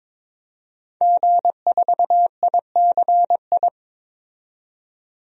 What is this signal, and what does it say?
Morse code, heard as a keyed tone, about 22 wpm, 700 Hz. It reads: G4ICI
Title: DXCC-100 Call Signs - 5 Characters - 22wpm